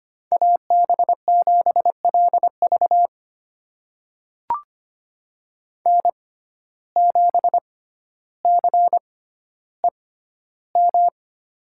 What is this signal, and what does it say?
Morse code, heard as a keyed tone, about 25 wpm, spaced as for 13 wpm, 700 Hz. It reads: AB7LV N7CEM
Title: USA Call Signs - 5 Characters - 6x Character Spacing - 25wpm